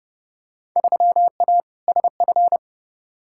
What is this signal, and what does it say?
Morse code, heard as a keyed tone, about 30 wpm, 700 Hz. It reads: 3A SF
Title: Field Day Exchanges Encore - 30wpm